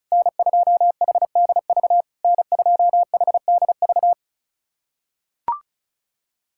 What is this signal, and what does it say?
Morse code, heard as a keyed tone, about 34 wpm, 690 Hz. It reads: N2HDV N2HDV
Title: SOTA QSO Practice 35wpm